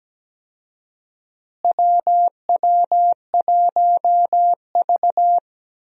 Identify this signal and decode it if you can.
Morse code, heard as a keyed tone, about 17 wpm, 700 Hz. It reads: WW1V
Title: USA Call Signs - 4 Characters - Repeated 2x - 17wpm